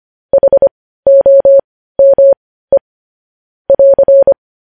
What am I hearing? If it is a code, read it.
Morse code, heard as a keyed tone, about 25 wpm, spaced as for 15 wpm, 550 Hz. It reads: HOME <AR>
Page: news headlines in morse code at 15 wpm